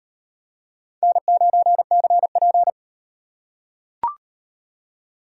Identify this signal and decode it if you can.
Morse code, heard as a keyed tone, about 35 wpm, 700 Hz. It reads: N9CP